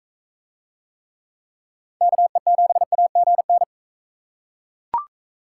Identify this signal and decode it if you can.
Morse code, heard as a keyed tone, about 38 wpm, 700 Hz. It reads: KE7AGN